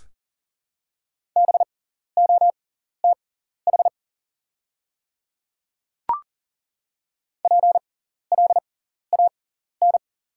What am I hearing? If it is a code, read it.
Morse code, heard as a keyed tone, about 40 wpm, spaced as for 14 wpm, 700 Hz. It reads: BOTH PLAN
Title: 6x Character Spacing - Top 500 Words - 40wpm